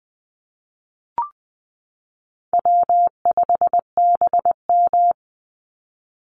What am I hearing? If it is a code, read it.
Morse code, heard as a keyed tone, about 20 wpm, 700 Hz. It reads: W5BM
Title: Easy Call Signs - 20wpm